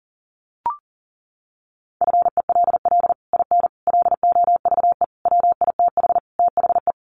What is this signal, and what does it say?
Morse code, heard as a keyed tone, about 40 wpm, 700 Hz. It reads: FELL IN LOVE WITH THE
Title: Sets of 5 Words Encore2 40wpm